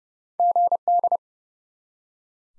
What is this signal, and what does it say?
Morse code, heard as a keyed tone, about 32 wpm, 700 Hz. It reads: GD
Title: Top 90 Abbr 30WPM